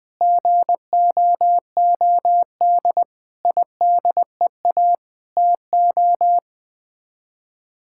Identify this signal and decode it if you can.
Morse code, heard as a keyed tone, about 20 wpm, 700 Hz. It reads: GOOD IDEA TO
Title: Sets of 8 Words - 20wpm